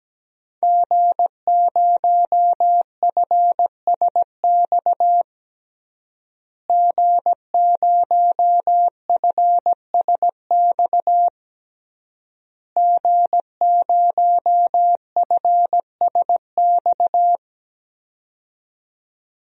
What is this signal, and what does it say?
Morse code, heard as a keyed tone, about 17 wpm, 700 Hz. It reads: G0FSX G0FSX G0FSX
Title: DXCC-100 Call Signs - 5 Characters - Repeated 3x - 17wpm